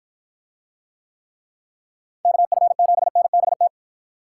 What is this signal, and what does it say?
Morse code, heard as a keyed tone, about 50 wpm, 700 Hz. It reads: KF7NBT